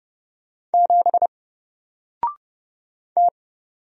Morse code, heard as a keyed tone, about 30 wpm, 700 Hz. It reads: 7 T
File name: Single-Letter-Number-30wpm